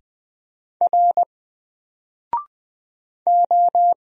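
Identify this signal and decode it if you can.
Morse code, heard as a keyed tone, about 20 wpm, 700 Hz. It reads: R O
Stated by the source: Single letter and number Morse drill at 20 wpm, from N4LCD.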